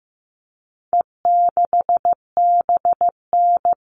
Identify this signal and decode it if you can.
Morse code, heard as a keyed tone, about 15 wpm, 700 Hz. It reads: E6BN
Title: All Cut Numbers - 4-Digit 15wpm